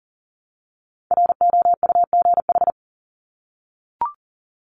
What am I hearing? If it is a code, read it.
Morse code, heard as a keyed tone, about 40 wpm, 700 Hz. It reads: ROUGH
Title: Top 1501-1750 Words 40wpm